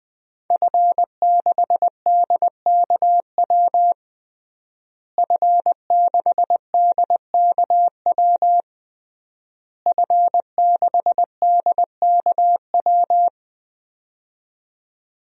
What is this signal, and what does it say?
Morse code, heard as a keyed tone, about 21 wpm, 700 Hz. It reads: F6DKW F6DKW F6DKW
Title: DXCC-100 Call Signs - 5 Characters - Repeated 3x - 20wpm